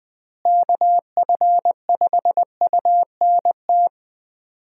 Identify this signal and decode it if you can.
Morse code, heard as a keyed tone, about 21 wpm, 700 Hz. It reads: KF5UNT
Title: USA Call Signs - 6 Characters - Repeated 3x - 20wpm